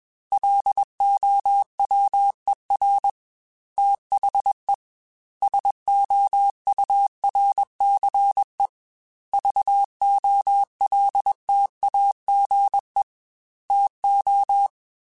Morse code, heard as a keyed tone, about 21 wpm, 800 Hz. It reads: LOWER THE SOURCE VOLTAGE TO
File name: Technical Subjects / Technical Subjects-mp3s / XC @20